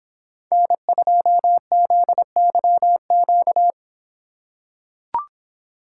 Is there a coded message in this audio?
Yes: Morse code from 0.5 to 3.7 s, heard as a keyed tone, about 26 wpm, 700 Hz. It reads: N2ZYQ